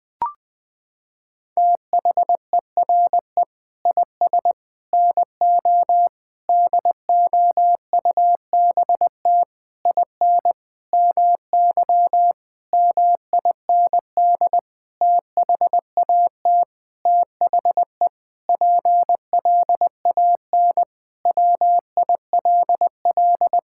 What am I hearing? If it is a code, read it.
Morse code, heard as a keyed tone, about 21 wpm, 700 Hz. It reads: THERE IS NO DOUBT IN MY MIND THAT THE PLAN WILL
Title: Sentences with 9-Word Phrases - 20wpm